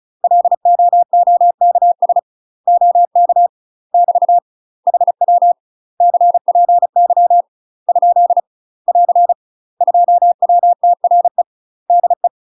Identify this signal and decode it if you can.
Morse code, heard as a keyed tone, about 34 wpm, 700 Hz. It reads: LOOKS OK <BT> HW CPY ? <AR> 2WTRE DE
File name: UK MASTER QSOs TRAINING / 35WPM / UK MASTER QSOs 0043